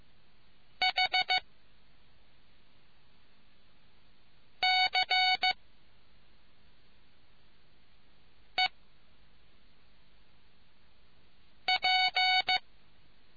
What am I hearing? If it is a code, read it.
Morse code, heard as a keyed tone, about 15 wpm, 740 Hz. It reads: H C E P